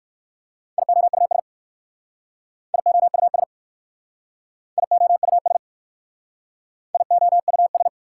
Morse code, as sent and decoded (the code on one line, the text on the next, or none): .. --- ..- ... / .. --- ..- ... / .. --- ..- ... / .. --- ..- ...
IOUS IOUS IOUS IOUS